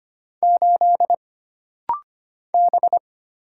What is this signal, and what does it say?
Morse code, heard as a keyed tone, about 25 wpm, 700 Hz. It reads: 8 B
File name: Single Character (Letter:Number:Punctuation:Prosign) - Rapid-Fire - 25wpm